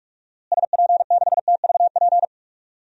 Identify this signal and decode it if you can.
Morse code, heard as a keyed tone, about 46 wpm, 700 Hz. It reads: SP6TVP
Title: DXCC-100 Call Signs - 6 Characters - Repeated 2x - 45wpm